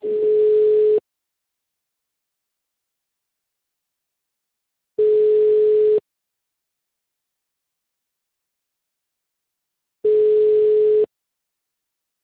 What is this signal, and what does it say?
Ringback tone (Europe, 425 Hz).
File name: out- -2007- - - .10948